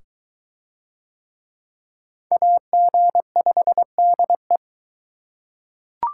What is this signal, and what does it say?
Morse code, heard as a keyed tone, about 22 wpm, 700 Hz. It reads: AG5DE